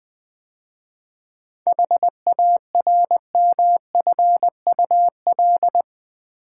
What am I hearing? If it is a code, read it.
Morse code, heard as a keyed tone, about 20 wpm, 700 Hz. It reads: HARMFUL
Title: 7 Letter Words - First 3 Letters Given - 20wpm